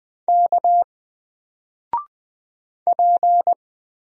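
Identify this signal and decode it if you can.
Morse code, heard as a keyed tone, about 20 wpm, 700 Hz. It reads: K P
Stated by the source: Single letter and number Morse drill at 20 wpm, from N4LCD.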